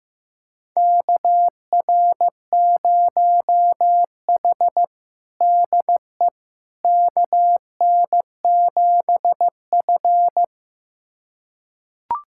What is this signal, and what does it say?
Morse code, heard as a keyed tone, about 15 wpm, 700 Hz. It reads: KR0H DE KN7F
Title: USA Callsign DE Callsign - 4 Characters - 15wpm